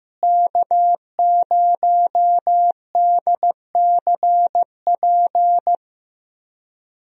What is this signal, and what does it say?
Morse code, heard as a keyed tone, about 15 wpm, 700 Hz. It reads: K0DCP
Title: USA Call Signs - 5 Characters - Repeated 2x - 15wpm